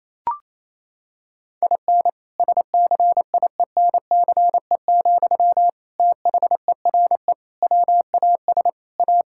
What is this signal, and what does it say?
Morse code, heard as a keyed tone, about 27 wpm, 700 Hz. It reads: IN SCIENCE, THERE WAS A
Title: Sentences from Top 1000 Words - Review Part 5 - 28wpm